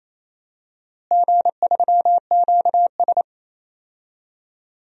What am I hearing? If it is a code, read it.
Morse code, heard as a keyed tone, about 28 wpm, 700 Hz. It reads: G3QS
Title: DXCC-100 Call Signs - 4 Characters - Repeated 3x - 28wpm